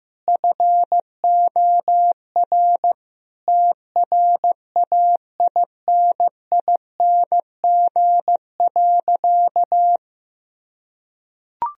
Morse code, heard as a keyed tone, about 15 wpm, 700 Hz. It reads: FOR TRAINING.
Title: Sentences from Top 1000 Words - Review Part 4 - 15wpm